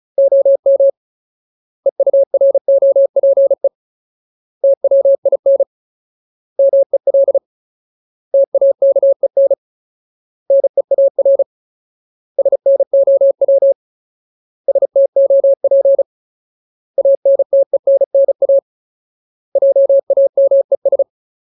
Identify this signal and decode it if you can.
Morse code, heard as a keyed tone, about 34 wpm, 550 Hz. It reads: OM EUROPE TWIN MEL TAKEN NEAR SNOW STOP ANTENNA JAMES
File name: Ultimate Word List / 35WPM Ultimate